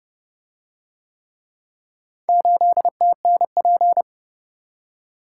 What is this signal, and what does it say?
Morse code, heard as a keyed tone, about 29 wpm, 700 Hz. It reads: 8TNP